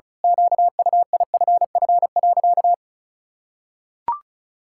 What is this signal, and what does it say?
Morse code, heard as a keyed tone, about 34 wpm, 700 Hz. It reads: QUIFF.
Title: Pangram Challenge v2 - 35wpm